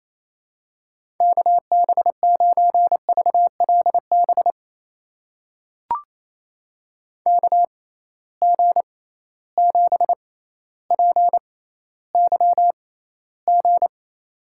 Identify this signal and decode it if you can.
Morse code, heard as a keyed tone, about 28 wpm, spaced as for 14 wpm, 700 Hz. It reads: KB9VLB KG7PYG
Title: USA Call Signs - 6 Characters - 6x Character Spacing - 28wpm